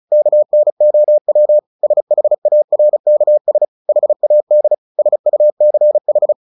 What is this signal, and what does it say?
Morse code, heard as a keyed tone, about 35 wpm, 600 Hz. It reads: KNOW SHARKS HAD SUCH
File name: the old man and the sea 2 35wpm